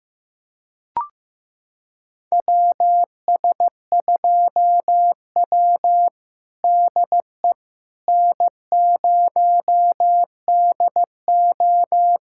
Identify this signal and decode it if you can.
Morse code, heard as a keyed tone, about 15 wpm, 700 Hz. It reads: WS2W DE N0DO
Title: USA Callsign DE Callsign - 4 Characters - 15wpm